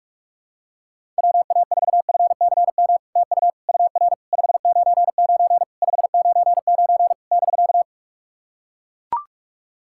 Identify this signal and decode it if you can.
Morse code, heard as a keyed tone, about 44 wpm, 700 Hz. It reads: WA4FCM TU UR 599 599 <BK>
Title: SOTA QSO Practice 45wpm